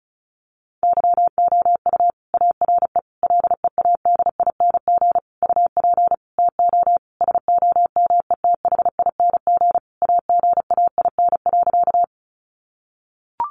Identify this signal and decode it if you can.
Morse code, heard as a keyed tone, about 34 wpm, 700 Hz. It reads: YOU ARE LEADING UP TO SOMETHING AGAIN.
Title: Sentences from Top 1100 Words 35wpm